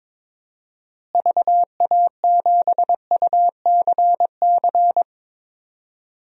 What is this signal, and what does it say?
Morse code, heard as a keyed tone, about 22 wpm, 700 Hz. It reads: VA7UCC
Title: DXCC-100 Call Signs - 6 Characters - Repeated 2x - 22wpm